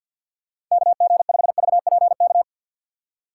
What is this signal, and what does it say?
Morse code, heard as a keyed tone, about 53 wpm, 700 Hz. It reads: KG5VPK
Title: USA Call Signs - 6 Characters - 3x Character Spacing - 50wpm